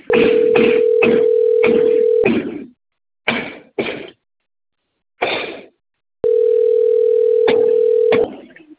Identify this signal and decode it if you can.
Ringback tone (North America), 440 + 480 Hz.